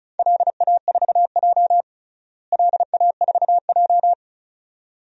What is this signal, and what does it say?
Morse code, heard as a keyed tone, about 36 wpm, 700 Hz. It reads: LA4J LA4J
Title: DXCC-100 Call Signs - 4 Characters - Repeated 2x - 35wpm